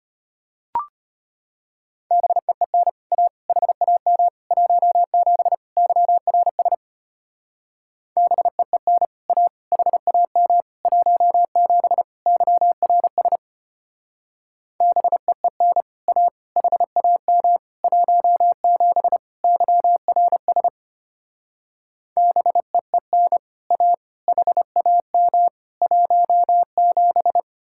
Morse code, heard as a keyed tone, about 29 wpm, 700 Hz. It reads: BEEN A HAM 17 YRS BEEN A HAM 17 YRS BEEN A HAM 17 YRS BEEN A HAM 17
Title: Speed-Racing Common QSO Phrases - 25wpm